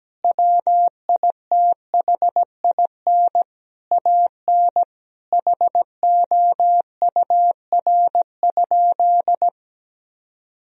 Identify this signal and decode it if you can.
Morse code, heard as a keyed tone, about 17 wpm, 700 Hz. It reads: WITHIN AN HOUR?
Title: Sentences from Top 1000 Words - Review Part 4 - 17wpm